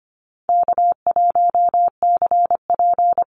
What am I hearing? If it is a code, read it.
Morse code, heard as a keyed tone, about 25 wpm, 700 Hz. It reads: K1CP
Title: Easy Call Signs Repeated 3x - 25wpm